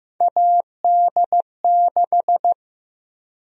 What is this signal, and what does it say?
Morse code, heard as a keyed tone, about 15 wpm, 700 Hz. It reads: AD6
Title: USA Call Sign Prefixes - 15wpm